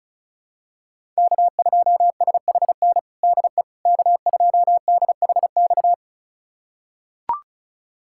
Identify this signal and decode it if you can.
Morse code, heard as a keyed tone, about 36 wpm, 700 Hz. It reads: K2SHN DE K2DHX
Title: USA Callsign DE Callsign - 5 Characters - 35wpm